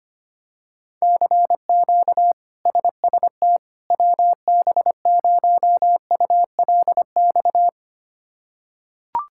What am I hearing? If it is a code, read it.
Morse code, heard as a keyed tone, about 25 wpm, 700 Hz. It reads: CQ SST WB0ULX
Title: K1USN SST QSO Practice - 25wpm